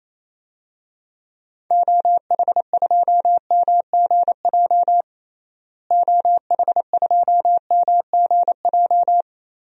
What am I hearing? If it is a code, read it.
Morse code, heard as a keyed tone, about 28 wpm, 700 Hz. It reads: OH2MGJ OH2MGJ
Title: DXCC-100 Call Signs - 6 Characters - Repeated 3x - 28wpm